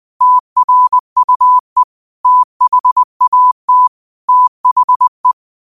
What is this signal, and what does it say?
Morse code, heard as a keyed tone, about 20 wpm, 1000 Hz. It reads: TRUE THAT THE
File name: 20wpm